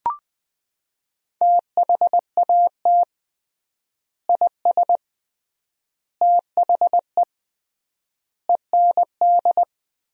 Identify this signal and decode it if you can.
Morse code, heard as a keyed tone, about 20 wpm, 700 Hz. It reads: THAT IS THE END
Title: 3x Word-Spacing - Sentences from Top 200 Words - 20wpm